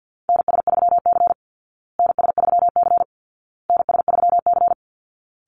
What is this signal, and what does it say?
Morse code, heard as a keyed tone, about 50 wpm, 700 Hz. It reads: NS3C NS3C NS3C
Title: Call Signs Repeated 2x - 50wpm